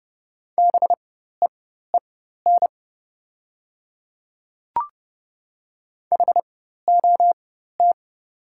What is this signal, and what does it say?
Morse code, heard as a keyed tone, about 28 wpm, spaced as for 14 wpm, 700 Hz. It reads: BEEN HOT
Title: 4x Character Spacing - Top 100 Words - 30wpm